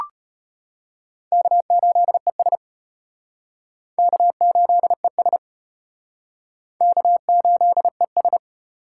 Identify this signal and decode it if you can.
Morse code, heard as a keyed tone, about 33 wpm, 700 Hz. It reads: K8ES K8ES K8ES